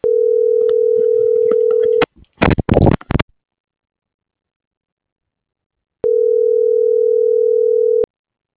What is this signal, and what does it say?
Ringback tone (North America), 440 + 480 Hz.